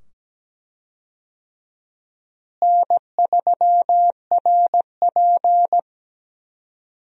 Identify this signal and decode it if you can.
Morse code, heard as a keyed tone, about 17 wpm, 700 Hz. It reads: N3RP
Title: USA Call Signs - 4 Characters - Repeated 2x - 17wpm